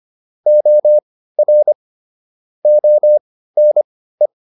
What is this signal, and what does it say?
Morse code, heard as a keyed tone, about 25 wpm, spaced as for 15 wpm, 600 Hz. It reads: OR ONE